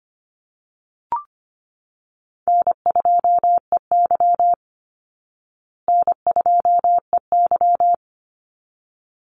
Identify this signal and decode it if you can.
Morse code, heard as a keyed tone, about 25 wpm, 700 Hz. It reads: N2EY N2EY
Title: Call Signs Repeated 3x - 25wpm